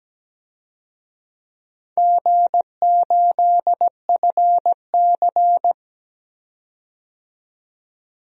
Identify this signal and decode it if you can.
Morse code, heard as a keyed tone, about 17 wpm, 700 Hz. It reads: G8FC
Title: DXCC-100 Call Signs - 4 Characters - 17wpm